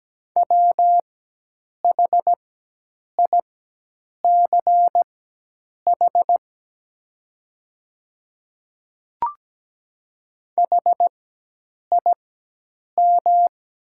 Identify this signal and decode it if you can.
Morse code, heard as a keyed tone, about 17 wpm, spaced as for 8 wpm, 700 Hz. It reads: WHICH HIM
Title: 4x Character Spacing - Top 100 Words - 17wpm